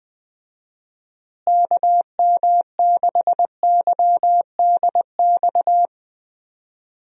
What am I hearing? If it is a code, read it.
Morse code, heard as a keyed tone, about 20 wpm, 700 Hz. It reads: KM6YDX
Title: USA Call Signs - 6 Characters - Repeated 3x - 20wpm